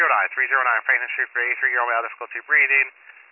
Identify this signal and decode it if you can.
Emergency Alert System (SAME) tones, heard as harsh buzzing bursts, with speech. It is heard from 0.3 to 2.9 s.